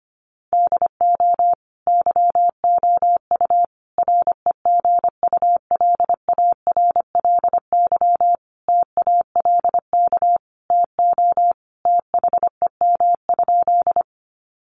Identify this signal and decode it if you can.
Morse code, heard as a keyed tone, about 25 wpm, 700 Hz. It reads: DO YOU REGULARLY TALK TO THEM?
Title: Sentences from Top 1400 Words 25wpm